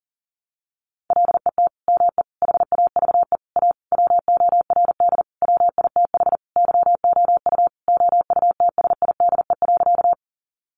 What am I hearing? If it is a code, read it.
Morse code, heard as a keyed tone, about 40 wpm, 700 Hz. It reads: LET ME HAVE A WORD WITH YOU OUTSIDE.
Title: Sentences from Top 900 Words 40wpm